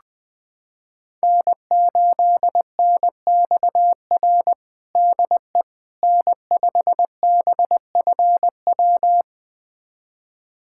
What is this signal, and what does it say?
Morse code, heard as a keyed tone, about 20 wpm, 700 Hz. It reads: N8NXR DE N5BFW
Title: USA Callsign DE Callsign - 5 Characters - 20wpm